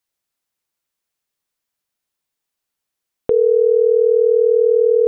Ringback tone (North America), 440 + 480 Hz.